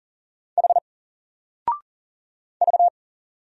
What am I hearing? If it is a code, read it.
Morse code, heard as a keyed tone, about 40 wpm, 700 Hz. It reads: H V
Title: Single Letter-Number 40wpm